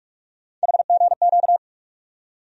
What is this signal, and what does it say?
Morse code, heard as a keyed tone, about 43 wpm, 700 Hz. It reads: HGQ
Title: USA Call Sign Postfixes - 3 Letters 45wpm